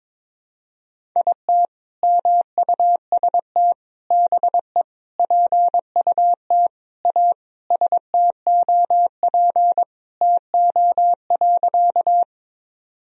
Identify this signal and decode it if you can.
Morse code, heard as a keyed tone, about 22 wpm, 700 Hz. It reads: IT MUST BE PUT A STOP TO.
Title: Sentences from Top 1000 Words - Review Part 1 - 22wpm